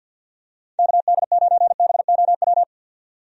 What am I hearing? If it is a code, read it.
Morse code, heard as a keyed tone, about 49 wpm, 700 Hz. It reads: KD9BOW
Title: USA Call Signs - 6 Characters - 50wpm